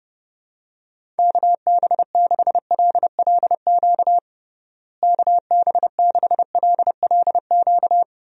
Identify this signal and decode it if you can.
Morse code, heard as a keyed tone, about 32 wpm, 700 Hz. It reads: KB6LLQ KB6LLQ
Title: USA Call Signs - 6 Characters - Repeated 3x - 30wpm